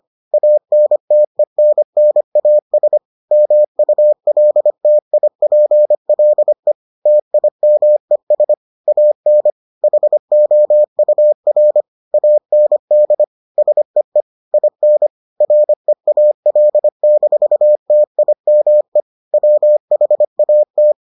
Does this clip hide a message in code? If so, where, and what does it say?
Yes: Morse code from 0.3 to 20.9 s, heard as a keyed tone, about 25 wpm, 600 Hz. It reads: ANTENNAS MULTIPLE TIMES AN HOUR AND SEE IN REAL-TIME WHAT